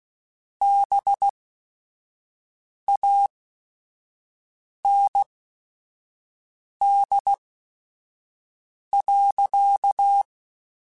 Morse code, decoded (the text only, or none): BAND.